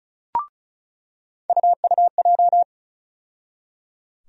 Morse code, heard as a keyed tone, about 36 wpm, 700 Hz. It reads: UUJ
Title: USA Call Sign Postfixes - 3 Letters 35wpm